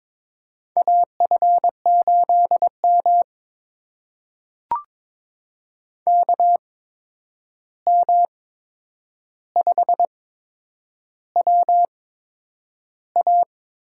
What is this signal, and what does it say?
Morse code, heard as a keyed tone, about 23 wpm, spaced as for 9 wpm, 700 Hz. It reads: AF8M KM5WA
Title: USA Call Signs - 8x Character Spacing - 22wpm